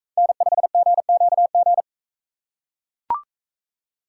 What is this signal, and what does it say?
Morse code, heard as a keyed tone, about 41 wpm, 700 Hz. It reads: N5GQG